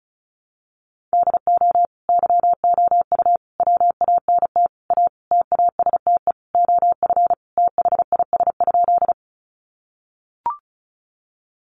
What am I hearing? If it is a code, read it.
Morse code, heard as a keyed tone, about 34 wpm, 700 Hz. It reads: DO YOU WANT A TASTE OF THIS?
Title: Sentences from Top 1100 Words 35wpm